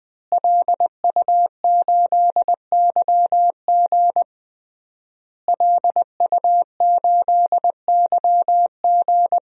Morse code, heard as a keyed tone, about 20 wpm, 700 Hz. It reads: LU8YG LU8YG
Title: DXCC-100 Call Signs - 5 Characters - Repeated 3x - 20wpm